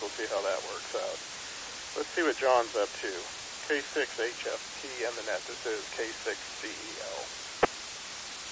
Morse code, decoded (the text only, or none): none